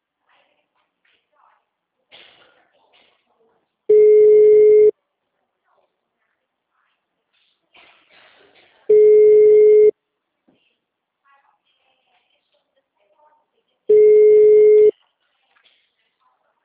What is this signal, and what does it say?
Ringback tone (Europe, 425 Hz), with speech.